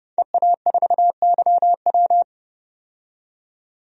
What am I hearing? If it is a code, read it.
Morse code, heard as a keyed tone, about 31 wpm, 700 Hz. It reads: EA4YW
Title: DXCC-100 Call Signs - 5 Characters - 30wpm